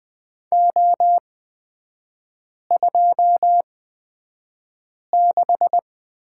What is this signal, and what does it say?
Morse code, heard as a keyed tone, about 20 wpm, spaced as for 5 wpm, 700 Hz. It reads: O26
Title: 250 Lesson 40e - BK - Three Characters 20wpm